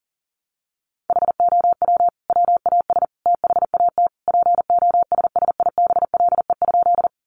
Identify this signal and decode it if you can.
Morse code, heard as a keyed tone, about 40 wpm, 700 Hz. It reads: HOW WAS THAT POSSIBLE?
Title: Sentences from Top 500 Words 40wpm